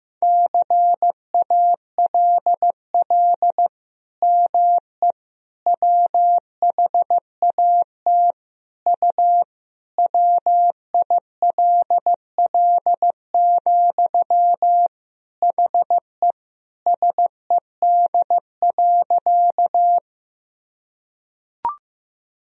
Morse code, heard as a keyed tone, about 15 wpm, 700 Hz. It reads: CALL ME WHAT U WILL, HE SED.
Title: Sentences from Top 200 Words and Abbreviations 15wpm